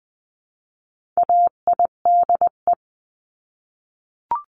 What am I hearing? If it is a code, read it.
Morse code, heard as a keyed tone, about 20 wpm, 700 Hz. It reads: AIDE
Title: 259 Lesson 42d - Words 18-20wpm